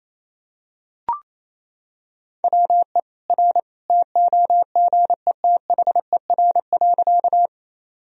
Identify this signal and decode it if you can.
Morse code, heard as a keyed tone, about 28 wpm, 700 Hz. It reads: WE R TOGETHER.
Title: Sentences from Top 300 Words and Abbreviations 28wpm